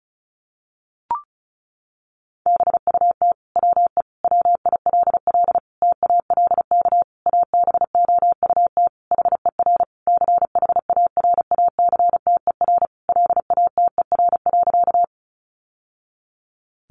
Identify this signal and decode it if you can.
Morse code, heard as a keyed tone, about 34 wpm, 700 Hz. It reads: BUT WE WILL TALK ABOUT HER CHARACTER LATER.
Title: Sentences from Top 1200 Words 35wpm